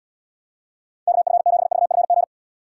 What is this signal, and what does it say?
Morse code, heard as a keyed tone, about 76 wpm, 700 Hz. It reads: KF7VFG